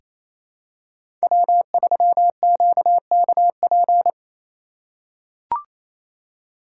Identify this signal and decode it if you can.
Morse code, heard as a keyed tone, about 28 wpm, 700 Hz. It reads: W3QKP
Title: USA Call Signs - 8x Character Spacing - 28wpm